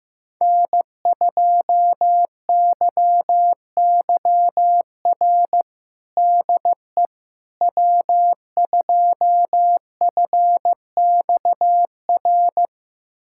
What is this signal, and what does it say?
Morse code, heard as a keyed tone, about 15 wpm, 700 Hz. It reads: N2YYR DE W2FXR
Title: USA Callsign DE Callsign - 5 Characters - 15wpm